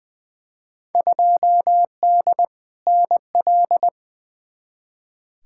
Morse code, heard as a keyed tone, about 20 wpm, 700 Hz. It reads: 2D NL
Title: Field Day Exchanges Encore - 20wpm